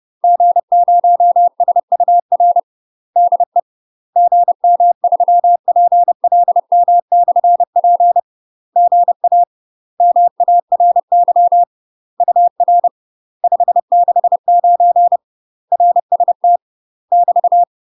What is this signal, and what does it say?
Morse code, heard as a keyed tone, about 30 wpm, 700 Hz. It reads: G0SUR DE GM3PLM/P GA MARY UR 569 RST <BT>
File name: UK MASTER QSOs TRAINING / 30WPM / UK MASTER QSOs 0050